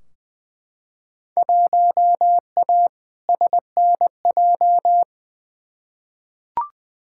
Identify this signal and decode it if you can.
Morse code, heard as a keyed tone, about 20 wpm, 700 Hz. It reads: 1A SNJ
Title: Field Day Exchanges - 20wpm